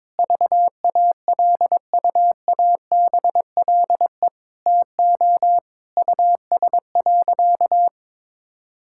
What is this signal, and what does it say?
Morse code, heard as a keyed tone, about 22 wpm, 700 Hz. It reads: VALUABLE TO US.
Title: Sentences from Top 1600 Words 22wpm